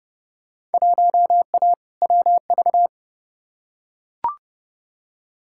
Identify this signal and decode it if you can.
Morse code, heard as a keyed tone, about 30 wpm, 700 Hz. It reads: 1A WV
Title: Field Day Exchanges Encore - 30wpm